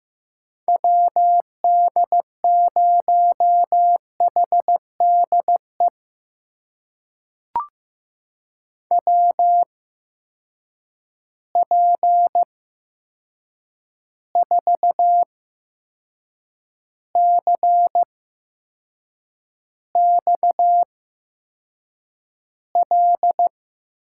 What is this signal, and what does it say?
Morse code, heard as a keyed tone, about 15 wpm, spaced as for 6 wpm, 700 Hz. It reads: WD0HDE WP4CXL
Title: USA Call Signs - 6 Characters - 8x Character Spacing - 15wpm